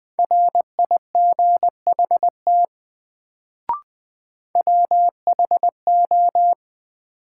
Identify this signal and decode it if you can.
Morse code, heard as a keyed tone, about 21 wpm, 700 Hz. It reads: RIGHT WHO
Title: Sets of 2 Words - New Word Drill - 20wpm